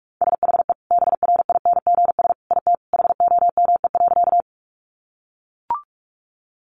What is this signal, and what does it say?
Morse code, heard as a keyed tone, about 44 wpm, 700 Hz. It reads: SHE BRINGS IT HOME.
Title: Easy Phrases 45wpm